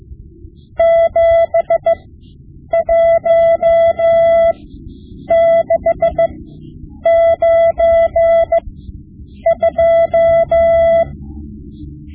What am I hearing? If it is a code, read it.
Morse code, heard as a keyed tone, about 15 wpm, spaced as for 8 wpm, 670 Hz. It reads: 71692
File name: Morse Lessons / Morse Lessons / Tape-5B